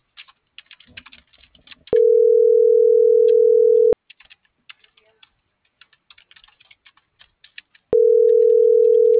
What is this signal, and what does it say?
Ringback tone (North America), 440 + 480 Hz.